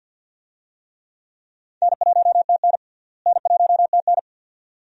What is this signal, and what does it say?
Morse code, heard as a keyed tone, about 46 wpm, 700 Hz. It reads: N1TN N1TN
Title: USA Call Signs - 4 Characters - Repeated 2x - 50wpm